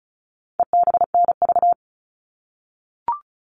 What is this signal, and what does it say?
Morse code, heard as a keyed tone, about 34 wpm, 700 Hz. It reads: EBNV
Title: All Cut Numbers - 4-Digit 35wpm